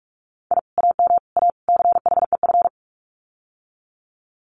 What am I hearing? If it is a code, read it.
Morse code, heard as a keyed tone, about 44 wpm, 700 Hz. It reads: I AM A CHEF
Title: Common QSO Phrases Encore 45wpm